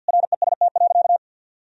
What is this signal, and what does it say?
Morse code, heard as a keyed tone, about 50 wpm, 700 Hz. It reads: REST.
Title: Sentences from Top 1000 Words - Review Part 4 - 50wpm